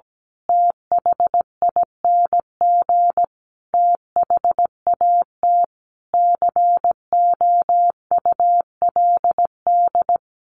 Morse code, heard as a keyed tone, about 17 wpm, 700 Hz. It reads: THING THAT COULD
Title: A Sound Of Thunder 17wpm